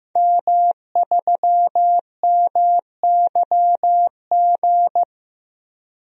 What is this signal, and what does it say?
Morse code, heard as a keyed tone, about 15 wpm, 700 Hz. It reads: M3MYG
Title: DXCC-100 Call Signs - 5 Characters - 15wpm